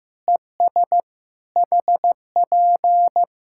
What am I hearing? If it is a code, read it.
Morse code, heard as a keyed tone, about 15 wpm, 700 Hz. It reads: ES HP
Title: Practice QSO - Exchange 4 - Ending - 15wpm